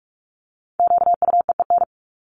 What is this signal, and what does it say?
Morse code, heard as a keyed tone, about 44 wpm, 700 Hz. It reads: QUEEN
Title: Top 1751-2000 Words 45wpm